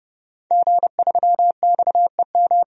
Morse code, heard as a keyed tone, about 30 wpm, 700 Hz. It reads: G3XEM
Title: DXCC-100 Call Signs - 5 Characters - 30wpm